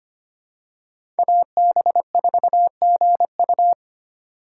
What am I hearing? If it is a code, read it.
Morse code, heard as a keyed tone, about 25 wpm, 700 Hz. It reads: AB4GU
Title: USA Call Signs - 5 Characters - 6x Character Spacing - 25wpm